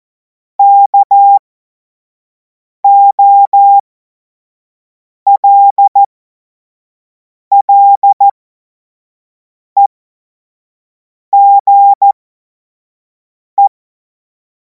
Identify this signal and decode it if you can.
Morse code, heard as a keyed tone, about 14 wpm, spaced as for 5 wpm, 800 Hz. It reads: KOLLEGE
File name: Morsekurs / Witze / Witz 17. März 2026 25 F70